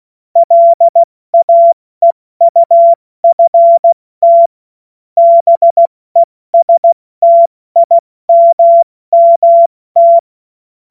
Morse code, heard as a keyed tone, about 16 wpm, 680 Hz. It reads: LAEUFT BESTIMMT
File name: Morsekurs / Witze / Witz 05 März 2024 70 F80